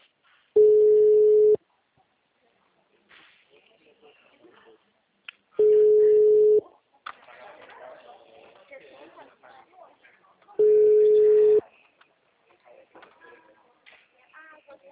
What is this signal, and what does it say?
Ringback tone (Europe, 425 Hz), with speech.
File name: out- -2007- - - .2464